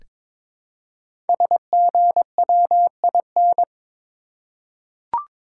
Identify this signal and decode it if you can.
Morse code, heard as a keyed tone, about 21 wpm, 700 Hz. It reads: SGWIN
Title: Code Groups - 5 Letters - 22wpm